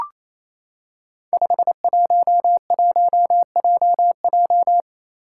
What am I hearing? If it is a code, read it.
Morse code, heard as a keyed tone, about 27 wpm, 700 Hz. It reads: 511JJ
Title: Five Challenging Letter-Number Combinations 28wpm